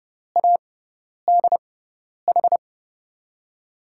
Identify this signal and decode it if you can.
Morse code, heard as a keyed tone, about 30 wpm, spaced as for 10 wpm, 700 Hz. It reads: ADH
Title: USA Call Signs - 6 Characters - 6x Character Spacing - 30wpm